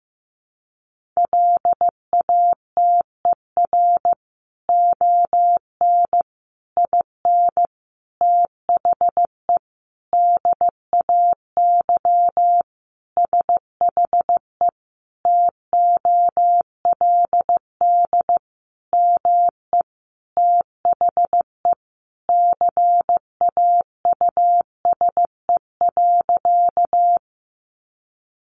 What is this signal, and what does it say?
Morse code, heard as a keyed tone, about 15 wpm, 700 Hz. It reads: LATER ON IN THE DAY SHE TOLD ME THE CAUSE.
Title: Sentences from Top 700 Words 15wpm